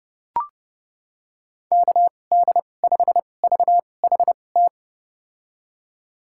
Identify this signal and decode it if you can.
Morse code, heard as a keyed tone, about 32 wpm, spaced as for 22 wpm, 700 Hz. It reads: KD5VHT
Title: USA Call Signs - 6 Characters - 2x Character Spacing - 30wpm